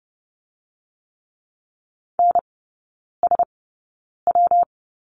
Morse code, heard as a keyed tone, about 30 wpm, 700 Hz. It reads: N S W
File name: Lesson 19e - Letter w - Three Characters 30wpm